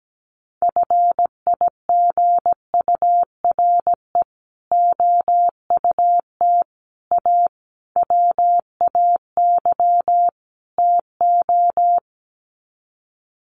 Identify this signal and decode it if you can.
Morse code, heard as a keyed tone, about 17 wpm, 700 Hz. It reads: FIGURE OUT A WAY TO
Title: Sets of 5 Words Encore2 17wpm